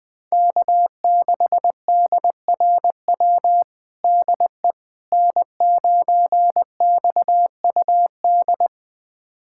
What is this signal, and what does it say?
Morse code, heard as a keyed tone, about 21 wpm, 700 Hz. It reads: K6DRW DE N9XUD
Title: USA Callsign DE Callsign - 5 Characters - 20wpm